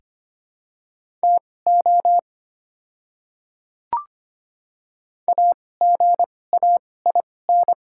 Morse code, heard as a keyed tone, about 25 wpm, spaced as for 18 wpm, 690 Hz. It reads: TO AGAIN
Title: 2x Character Spacing - Top 300 Words - 25wpm